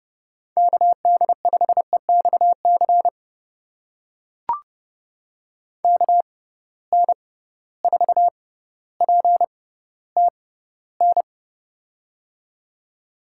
Morse code, heard as a keyed tone, about 30 wpm, spaced as for 15 wpm, 700 Hz. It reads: KD5EXC KN4PTN
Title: USA Call Signs - 6 Characters - 6x Character Spacing - 30wpm